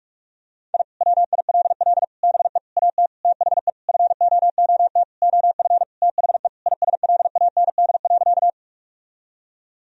Morse code, heard as a keyed tone, about 46 wpm, 700 Hz. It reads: I WILL BE AT THE FOOT OF THE ISLAND.
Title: Sentences from Top 1000 Words - Review Part 3 - 45wpm